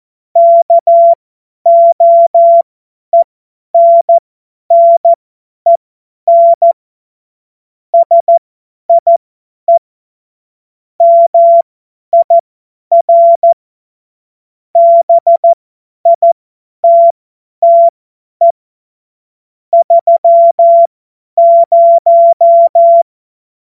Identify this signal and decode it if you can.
Morse code, heard as a keyed tone, about 14 wpm, spaced as for 10 wpm, 680 Hz. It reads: KOENNEN SIE MIR BITTE 30
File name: Witz-23-September-2025-50-F70